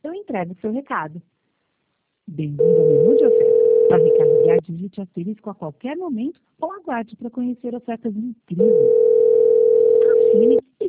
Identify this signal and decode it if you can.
Ringback tone (North America), 440 + 480 Hz, with speech.
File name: out- -2007- - - .963